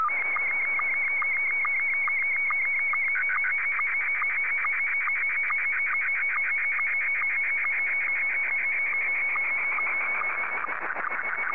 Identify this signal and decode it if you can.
SSTV (slow-scan television), Scottie 1.
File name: sstv / S1-VK2EX